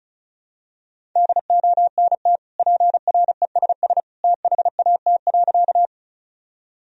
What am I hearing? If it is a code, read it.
Morse code, heard as a keyed tone, about 36 wpm, 700 Hz. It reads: DONT PRESS THAT.
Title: Sentences from Top 1000 Words - Review Part 1 - 35wpm